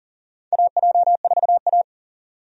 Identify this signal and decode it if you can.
Morse code, heard as a keyed tone, about 40 wpm, 700 Hz. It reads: AJ4A
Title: USA Call Signs - 4 Characters - 40wpm